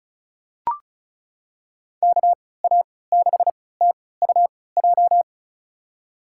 Morse code, heard as a keyed tone, about 36 wpm, spaced as for 20 wpm, 700 Hz. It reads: KA6TUJ
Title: USA Call Signs - 3x Character Spacing - 35wpm